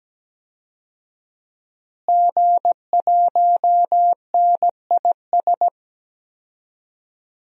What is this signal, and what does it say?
Morse code, heard as a keyed tone, about 17 wpm, 700 Hz. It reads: G1NIS
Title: DXCC-100 Call Signs - 5 Characters - Repeated 3x - 17wpm